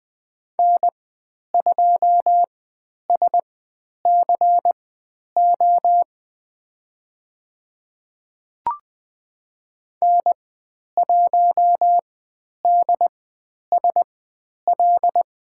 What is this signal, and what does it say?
Morse code, heard as a keyed tone, about 20 wpm, spaced as for 10 wpm, 700 Hz. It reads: N2SCO N1DSL
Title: 095 Lesson 16g - Letter c - Call Signs 20wpm